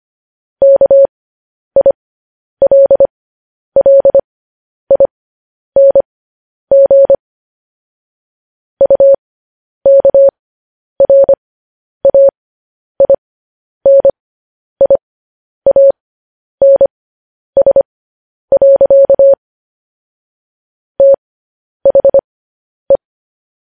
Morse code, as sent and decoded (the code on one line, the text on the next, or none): -.- .. .-.. .-.. .. -. --. / ..- -.- .-. .- .. -. .. .- -. ... .-.-.- / - .... .
KILLING UKRAINIANS. THE